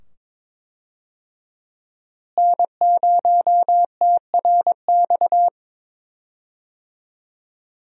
Morse code, heard as a keyed tone, about 21 wpm, 700 Hz. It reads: N0TRX
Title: USA Call Signs - 5 Characters - Repeated 2x - 22wpm